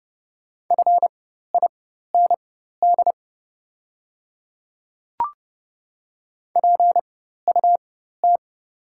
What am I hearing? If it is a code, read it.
Morse code, heard as a keyed tone, about 30 wpm, spaced as for 14 wpm, 700 Hz. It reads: FIND PUT
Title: 4x Character Spacing - Top 500 Words - 30wpm